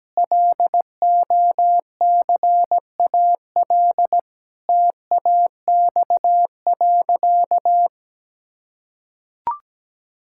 Morse code, heard as a keyed tone, about 17 wpm, 700 Hz. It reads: LOCAL TAX.
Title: Sentences from Top 1000 Words - Review Part 5 - 17wpm